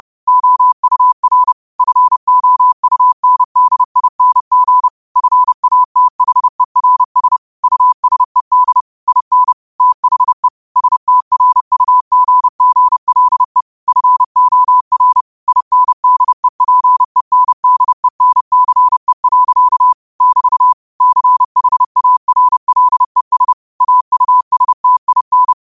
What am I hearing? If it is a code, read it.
Morse code, heard as a keyed tone, about 30 wpm, 1000 Hz. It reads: OUR FOUNDING FATHERS USED IN THE STRUGGLE FOR INDEPENDENCE. <BT> CHARLES AUSTIN